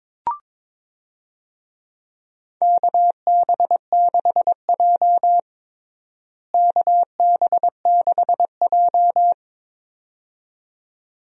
Morse code, heard as a keyed tone, about 21 wpm, 700 Hz. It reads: KB6J KB6J
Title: USA Call Signs - 4 Characters - Repeated 2x - 22wpm